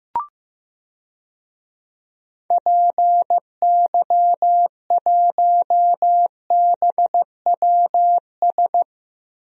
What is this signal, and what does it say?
Morse code, heard as a keyed tone, about 15 wpm, 700 Hz. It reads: PY1BWS
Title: DXCC-100 Call Signs - 6 Characters - 15wpm